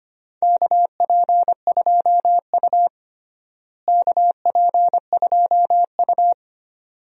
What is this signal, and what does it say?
Morse code, heard as a keyed tone, about 25 wpm, 700 Hz. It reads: KP2U KP2U
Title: DXCC-100 Call Signs - 4 Characters - Repeated 2x - 25wpm